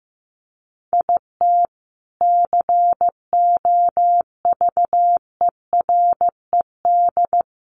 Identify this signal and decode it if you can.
Morse code, heard as a keyed tone, about 15 wpm, 700 Hz. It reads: IT COVERED
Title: A Sound Of Thunder 15wpm